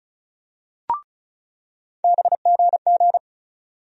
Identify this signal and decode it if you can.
Morse code, heard as a keyed tone, about 36 wpm, 700 Hz. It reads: BGG
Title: USA Call Sign Postfixes - 3 Letters 35wpm